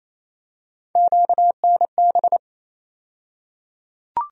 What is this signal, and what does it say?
Morse code, heard as a keyed tone, about 28 wpm, 700 Hz. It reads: QNB
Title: USA Call Sign Postfixes - 3 Letters 28wpm